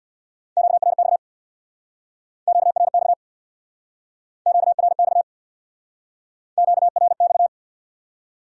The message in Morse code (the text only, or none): QRX QRX QRX QRX